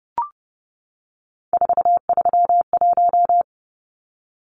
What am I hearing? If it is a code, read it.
Morse code, heard as a keyed tone, about 30 wpm, 700 Hz. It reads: 431
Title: Numbers - 3-Digit 30wpm